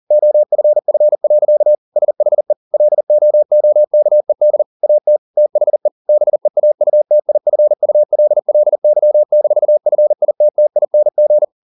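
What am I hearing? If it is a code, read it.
Morse code, heard as a keyed tone, about 40 wpm, 600 Hz. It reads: OFF. SHE LOOKED AT THE BEAUTIFULLY-FITTING
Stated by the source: Morse code at 40 wpm, practice text, CW med Gnister.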